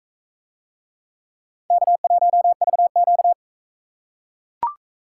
Morse code, heard as a keyed tone, about 42 wpm, 700 Hz. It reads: K1VQ